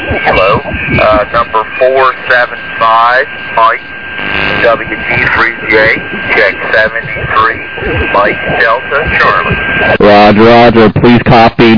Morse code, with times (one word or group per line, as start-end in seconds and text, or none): none